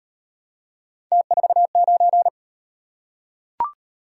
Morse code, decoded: T49

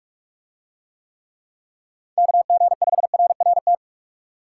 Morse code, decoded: KG5RRT

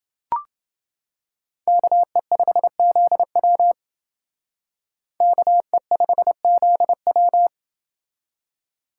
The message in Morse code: KE5ZW KE5ZW

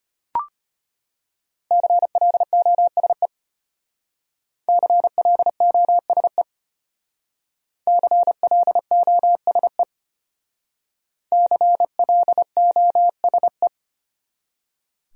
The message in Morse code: CLOSE CLOSE CLOSE CLOSE

